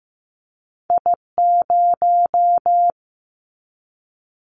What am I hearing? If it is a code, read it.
Morse code, heard as a keyed tone, about 15 wpm, 700 Hz. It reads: I0
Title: Callsign Prefixes for Top 25 DXCC - 15wpm